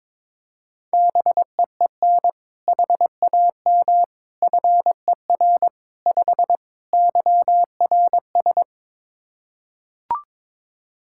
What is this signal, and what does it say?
Morse code, heard as a keyed tone, about 22 wpm, 700 Hz. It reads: BEEN HAM FER 5 YRS
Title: Practice QSO - Exchange 3 - More Info - 22wpm